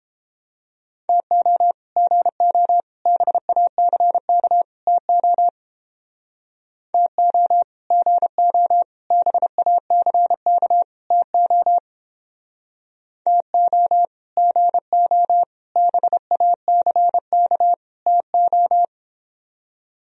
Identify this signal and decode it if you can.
Morse code, heard as a keyed tone, about 29 wpm, 700 Hz. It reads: TO GO BACK TO TO GO BACK TO TO GO BACK TO